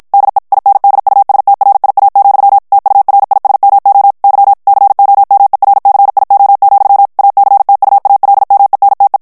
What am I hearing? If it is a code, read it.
Morse code, heard as a keyed tone, about 50 wpm, 800 Hz. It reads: DE INDUSTRIA, TURISMO Y COMERCIO, ACTUALMENTE
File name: ESPAGNOL TEXTE EN CLAIR 2 VITESSE 50 MOTS MINUTES